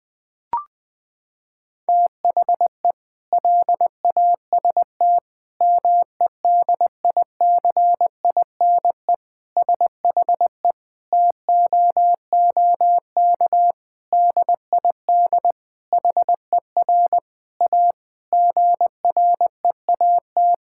Morse code, heard as a keyed tone, about 20 wpm, 700 Hz. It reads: THE LAST MEDICINE SHE TOOK DID HER A GREAT
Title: Sentences from Top 1000 Words - Review Part 6 - 20wpm